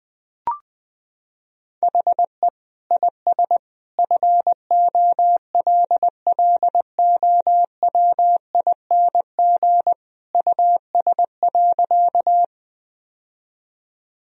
Morse code, heard as a keyed tone, about 20 wpm, 700 Hz. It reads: HE IS FOLLOWING US.
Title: Sentences from Top 1000 Words - Review Part 4 - 20wpm